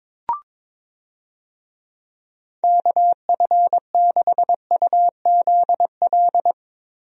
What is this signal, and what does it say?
Morse code, heard as a keyed tone, about 22 wpm, 700 Hz. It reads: KF6UZL
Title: USA Call Signs - 6 Characters - Repeated 2x - 22wpm